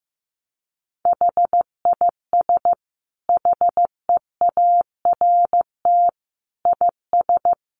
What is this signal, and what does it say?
Morse code, heard as a keyed tone, about 15 wpm, 700 Hz. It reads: HIS HEART IS
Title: Sentences from Top 1200 Words 15wpm